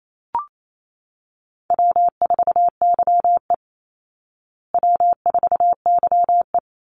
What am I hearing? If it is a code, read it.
Morse code, heard as a keyed tone, about 28 wpm, 690 Hz. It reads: W4YE W4YE
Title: Call Signs Repeated 3x - 28wpm